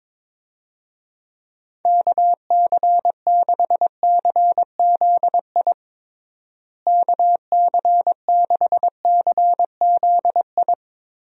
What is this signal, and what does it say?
Morse code, heard as a keyed tone, about 22 wpm, 700 Hz. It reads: KC6CZI KC6CZI
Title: USA Call Signs - 6 Characters - Repeated 2x - 22wpm